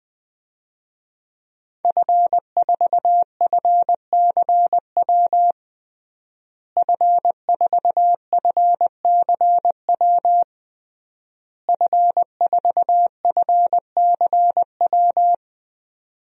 Morse code, heard as a keyed tone, about 20 wpm, 700 Hz. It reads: F4FCW F4FCW F4FCW
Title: DXCC-100 Call Signs - 5 Characters - Repeated 3x - 20wpm